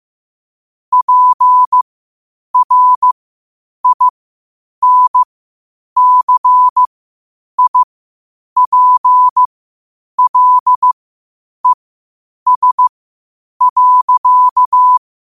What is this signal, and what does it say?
Morse code, heard as a keyed tone, about 15 wpm, spaced as for 8 wpm, 1000 Hz. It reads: PRINCIPLES.
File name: qotdfiles / 8wpm